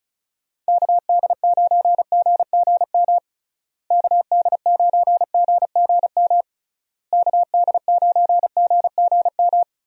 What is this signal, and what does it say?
Morse code, heard as a keyed tone, about 36 wpm, 700 Hz. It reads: KD9GGM KD9GGM KD9GGM
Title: USA Call Signs - 6 Characters - Repeated 3x - 35wpm